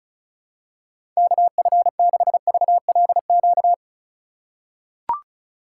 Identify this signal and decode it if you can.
Morse code, heard as a keyed tone, about 35 wpm, 690 Hz. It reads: KF6VLQ
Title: USA Call Signs - 6 Characters - 3x Character Spacing - 35wpm